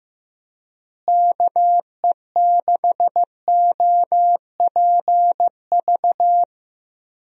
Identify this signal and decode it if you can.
Morse code, heard as a keyed tone, about 15 wpm, 700 Hz. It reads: KE6OPV
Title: USA Call Signs - 6 Characters - Repeated 2x - 15wpm